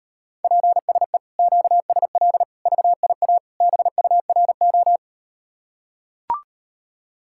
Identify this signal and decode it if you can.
Morse code, heard as a keyed tone, about 38 wpm, 700 Hz. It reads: PSE QSL VIA BURO